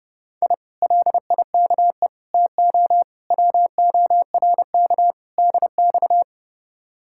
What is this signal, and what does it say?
Morse code, heard as a keyed tone, about 35 wpm, 700 Hz. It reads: I LIKE TO WORK DX
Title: Speed-Racing Common QSO Phrases - 30wpm